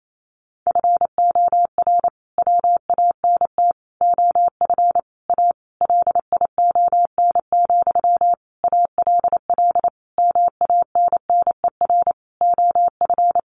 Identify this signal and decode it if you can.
Morse code, heard as a keyed tone, about 28 wpm, 700 Hz. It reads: FOR WANT OF A LION, ALL MANNER OF
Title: A Sound Of Thunder 28wpm